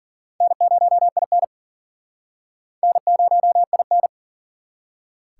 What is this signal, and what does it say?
Morse code, heard as a keyed tone, about 51 wpm, 700 Hz. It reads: N0IN N0IN